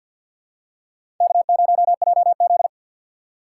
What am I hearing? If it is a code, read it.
Morse code, heard as a keyed tone, about 53 wpm, 700 Hz. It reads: K0JZ
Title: USA Call Signs - 4x Character Spacing - 50wpm